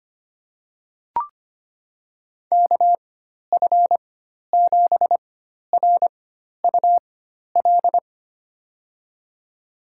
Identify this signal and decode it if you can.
Morse code, heard as a keyed tone, about 25 wpm, spaced as for 12 wpm, 700 Hz. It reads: KF7RUL
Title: USA Call Signs - 6 Characters - 4x Character Spacing - 25wpm